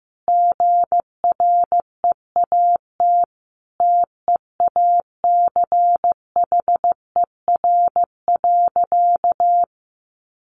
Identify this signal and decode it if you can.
Morse code, heard as a keyed tone, about 15 wpm, 700 Hz. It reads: GREAT TEACHER.
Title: Sentences from Top 900 Words 15wpm